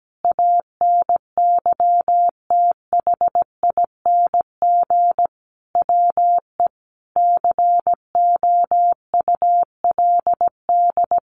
Morse code, heard as a keyed tone, about 17 wpm, 700 Hz. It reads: ANYTHING WE COULD
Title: Sentences from Top 1400 Words 17wpm